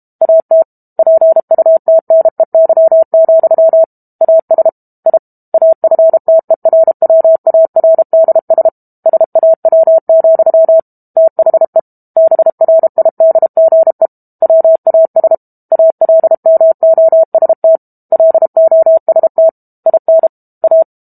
Morse code, heard as a keyed tone, about 32 wpm, 650 Hz. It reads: AT PUTNEY, AS I AFTERWARDS SAW, THE BRIDGE WAS ALMOST LOST IN A
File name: Book Two / 06-Chapter 6 - 32 WPM